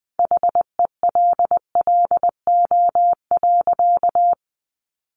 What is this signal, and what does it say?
Morse code, heard as a keyed tone, about 20 wpm, 700 Hz. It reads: HELLO.
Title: Sentences from Top 1100 Words 20wpm